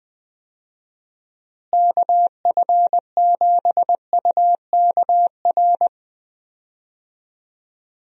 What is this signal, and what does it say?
Morse code, heard as a keyed tone, about 20 wpm, 700 Hz. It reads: KF7UKR
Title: USA Call Signs - 6 Characters - Repeated 2x - 20wpm